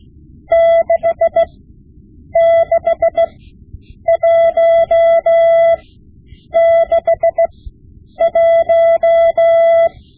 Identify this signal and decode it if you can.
Morse code, heard as a keyed tone, about 16 wpm, spaced as for 8 wpm, 670 Hz. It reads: 66161
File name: Morse Lessons / Morse Lessons / Tape-5B